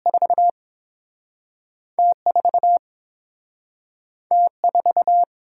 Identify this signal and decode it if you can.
Morse code, heard as a keyed tone, about 31 wpm, 700 Hz. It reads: 4 T 4 T 4